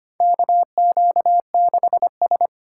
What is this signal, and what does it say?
Morse code, heard as a keyed tone, about 25 wpm, 700 Hz. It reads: KQ6S
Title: USA Call Signs - 4 Characters - 3x Character Spacing - 25wpm